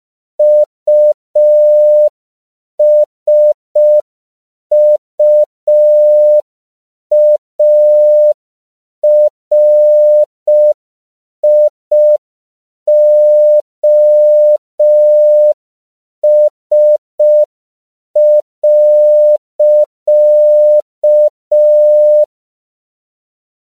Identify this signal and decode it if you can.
Morse code, heard as a keyed tone, about 5 wpm, 600 Hz. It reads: USUARIOS.
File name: CW / 5PPM